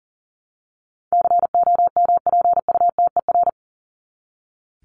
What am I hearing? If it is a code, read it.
Morse code, heard as a keyed tone, about 40 wpm, 700 Hz. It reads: COMPUTER
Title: Top 501-750 Words 40wpm